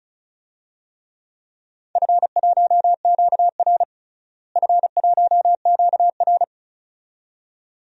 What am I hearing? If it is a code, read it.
Morse code, heard as a keyed tone, about 36 wpm, 700 Hz. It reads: F1QR F1QR
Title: DXCC-100 Call Signs - 4 Characters - Repeated 2x - 35wpm